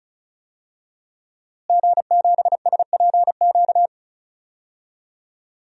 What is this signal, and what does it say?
Morse code, heard as a keyed tone, about 36 wpm, 700 Hz. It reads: G7SPQ
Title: DXCC-100 Call Signs - 5 Characters - Repeated 2x - 35wpm